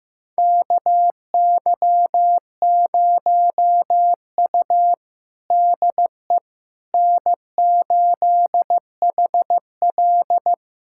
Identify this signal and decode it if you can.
Morse code, heard as a keyed tone, about 16 wpm, 700 Hz. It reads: KY0U DE N8HL
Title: USA Callsign DE Callsign - 4 Characters - 15wpm